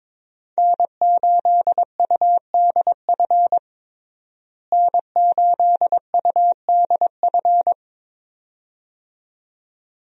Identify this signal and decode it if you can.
Morse code, heard as a keyed tone, about 22 wpm, 700 Hz. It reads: N8UDF N8UDF
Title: USA Call Signs - 5 Characters - Repeated 2x - 22wpm